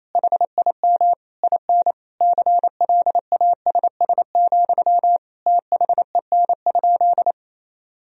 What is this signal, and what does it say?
Morse code, heard as a keyed tone, about 28 wpm, 700 Hz. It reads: HIM IN CLASS, THEN?
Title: Sentences from Top 1000 Words - Review Part 4 - 28wpm